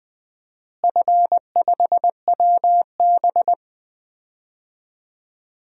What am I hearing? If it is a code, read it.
Morse code, heard as a keyed tone, about 20 wpm, 700 Hz. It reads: F5WB
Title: DXCC-100 Call Signs - 4 Characters - Repeated 2x - 20wpm